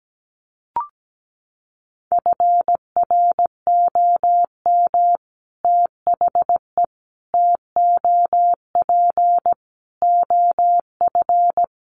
Morse code, heard as a keyed tone, about 17 wpm, 700 Hz. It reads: FROM THE TOP OF
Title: Sets of 5 Words Encore 17wpm